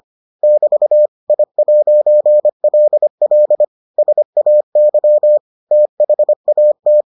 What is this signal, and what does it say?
Morse code, heard as a keyed tone, about 25 wpm, 600 Hz. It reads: <BT> I'LL SAY THAT